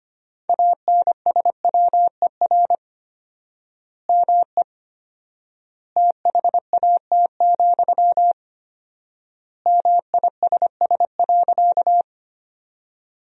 Morse code, heard as a keyed tone, about 25 wpm, 700 Hz. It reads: ANSWER ME THAT, MISS.
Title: 4x Word-Spacing - Sentences from Top 500 Words - 25wpm